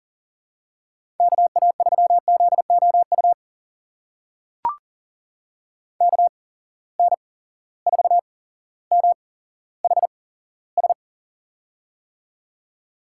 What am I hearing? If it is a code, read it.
Morse code, heard as a keyed tone, about 40 wpm, spaced as for 17 wpm, 700 Hz. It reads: KA3ZOU KN4MHS
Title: USA Call Signs - 6 Characters - 8x Character Spacing - 40wpm